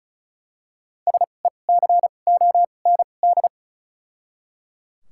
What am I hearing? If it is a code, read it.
Morse code, heard as a keyed tone, about 35 wpm, spaced as for 25 wpm, 700 Hz. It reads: SECOND
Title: 2x Character Spacing - Top 300 Words - 35wpm